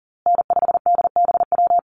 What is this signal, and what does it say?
Morse code, heard as a keyed tone, about 40 wpm, 700 Hz. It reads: N5DBW
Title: USA Call Signs Repeated 2x 40wpm